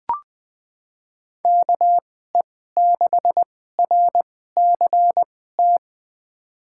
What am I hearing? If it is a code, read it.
Morse code, heard as a keyed tone, about 20 wpm, spaced as for 14 wpm, 700 Hz. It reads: KE6RCT
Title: USA Call Signs - 2x Character Spacing - 20wpm